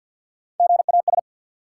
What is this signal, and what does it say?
Morse code, heard as a keyed tone, about 52 wpm, 700 Hz. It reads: GAS
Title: Sentences from Top 1100 Words - New Word Drill - 50wpm